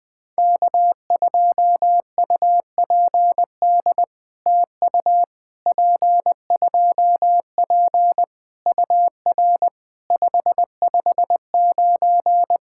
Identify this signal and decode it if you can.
Morse code, heard as a keyed tone, about 20 wpm, 700 Hz. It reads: K2UPD TU P2P UR 559
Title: POTA QSO Practice - 20wpm